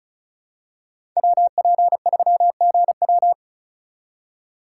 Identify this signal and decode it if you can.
Morse code, heard as a keyed tone, about 36 wpm, 700 Hz. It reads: WP3GW
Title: USA Call Signs - 5 Characters - 4x Character Spacing - 35wpm